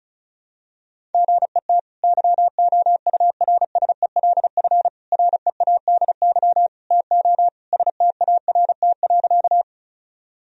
Morse code, heard as a keyed tone, about 34 wpm, 700 Hz. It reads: GET YOURSELF READY TO START.
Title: Sentences from Top 1000 Words - Review Part 1 - 35wpm